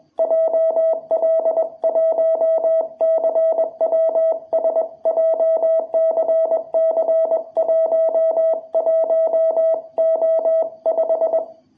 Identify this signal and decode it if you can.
Morse code, heard as a keyed tone, about 21 wpm, 660 Hz. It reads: JL1CWS JCC11O5
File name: jh8cbh / 2021cw